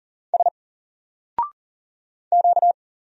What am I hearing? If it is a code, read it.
Morse code, heard as a keyed tone, about 39 wpm, 700 Hz. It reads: S Q
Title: Single Character (Letter:Number:Punctuation:Prosign) 40wpm